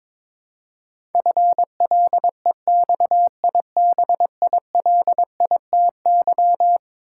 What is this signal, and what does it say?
Morse code, heard as a keyed tone, about 22 wpm, 700 Hz. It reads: FLEXIBILITY
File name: practice / 22wpm / Eleven Letter Words - 22wpm